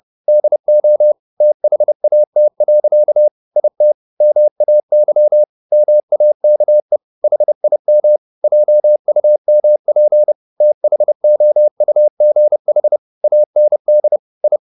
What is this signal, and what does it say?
Morse code, heard as a keyed tone, about 30 wpm, 600 Hz. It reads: DO THAT. IT MAY MAKE HIM JUMP THOUGH AND I